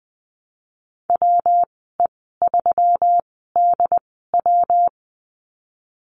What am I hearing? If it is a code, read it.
Morse code, heard as a keyed tone, about 20 wpm, spaced as for 14 wpm, 700 Hz. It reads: WE3DW
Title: 140 Lesson 23g - Letter f - Call Signs 20wpm